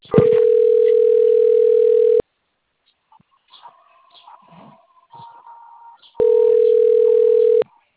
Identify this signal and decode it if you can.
Ringback tone (North America), 440 + 480 Hz.